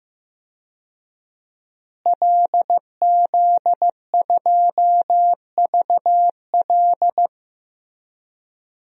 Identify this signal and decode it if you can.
Morse code, heard as a keyed tone, about 15 wpm, 700 Hz. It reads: LZ2VL
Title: DXCC-100 Call Signs - 5 Characters - Repeated 3x - 15wpm